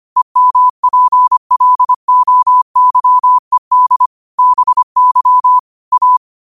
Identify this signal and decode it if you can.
Morse code, heard as a keyed tone, about 25 wpm, 1000 Hz. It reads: EMPLOYED BY A
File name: qotdfiles / 25wpm